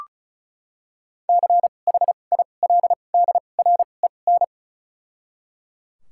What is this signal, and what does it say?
Morse code, heard as a keyed tone, about 36 wpm, 700 Hz. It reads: C H I L D R E N
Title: 2x Character Spacing - Top 500 Words - 35wpm